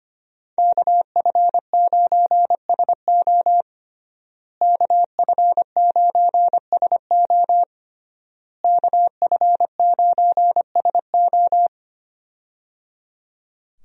Morse code, heard as a keyed tone, about 25 wpm, 700 Hz. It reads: KF9SO KF9SO KF9SO
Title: USA Call Signs - 5 Characters - Repeated 3x - 25wpm